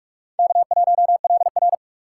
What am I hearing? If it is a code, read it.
Morse code, heard as a keyed tone, about 60 wpm, spaced as for 53 wpm, 700 Hz. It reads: K1LR
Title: Speed Racing - USA Call Signs 45wpm